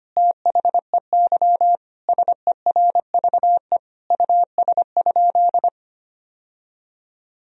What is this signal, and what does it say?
Morse code, heard as a keyed tone, about 25 wpm, 700 Hz. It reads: THEY SERVE US?
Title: Sentences from Top 1000 Words - Review Part 2 - 25wpm